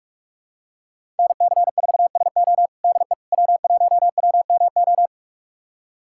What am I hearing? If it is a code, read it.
Morse code, heard as a keyed tone, about 46 wpm, 700 Hz. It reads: NC4SO DE W1WMO
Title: USA Callsign DE Callsign - 5 Characters - 45wpm